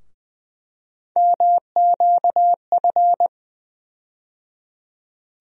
Morse code, heard as a keyed tone, about 20 wpm, 700 Hz. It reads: MQF
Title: USA Call Sign Postfixes - 3 Letters 20wpm